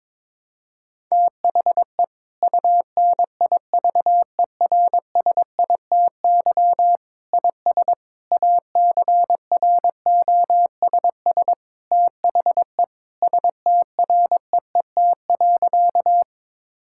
Morse code, heard as a keyed tone, about 22 wpm, 700 Hz. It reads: THE UNIVERSITY IS ACROSS THE STREET.
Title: Sentences from Top 1000 Words - Review Part 5 - 22wpm